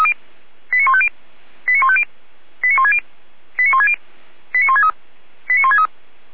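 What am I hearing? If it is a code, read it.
Five-tone selective call (ZVEI), heard as quick runs of five tones, tones of about 70 ms. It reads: 89150 89150 89160 89160 89163 89163
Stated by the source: ZVEI five-tone call 89121 received by SDR.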